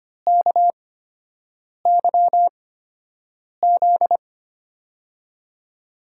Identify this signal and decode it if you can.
Morse code, heard as a keyed tone, about 25 wpm, spaced as for 7 wpm, 700 Hz. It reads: KYZ
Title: USA Call Signs - 8x Character Spacing - 25wpm